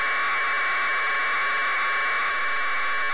Packet radio (AX.25, as in APRS).